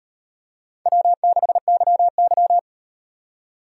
Morse code, heard as a keyed tone, about 36 wpm, 700 Hz. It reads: W6YY